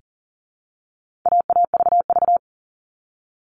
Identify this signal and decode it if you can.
Morse code, heard as a keyed tone, about 40 wpm, 700 Hz. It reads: AAVV
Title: All Cut Numbers - 4-Digit 40wpm